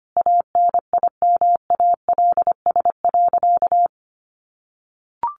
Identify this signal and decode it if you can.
Morse code, heard as a keyed tone, about 25 wpm, 700 Hz. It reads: ANIMALS.
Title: A Sound Of Thunder 25wpm